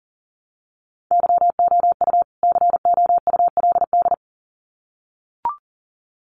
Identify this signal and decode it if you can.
Morse code, heard as a keyed tone, about 40 wpm, 700 Hz. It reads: YOU COULD
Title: Sets of 2 Words 40wpm